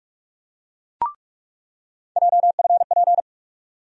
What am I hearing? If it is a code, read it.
Morse code, heard as a keyed tone, about 43 wpm, 700 Hz. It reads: JFP
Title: USA Call Sign Postfixes - 3 Letters 45wpm